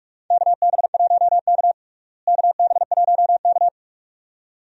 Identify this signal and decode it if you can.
Morse code, heard as a keyed tone, about 47 wpm, 700 Hz. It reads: KB1K KB1K
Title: USA Call Signs - 4 Characters - Repeated 2x - 45wpm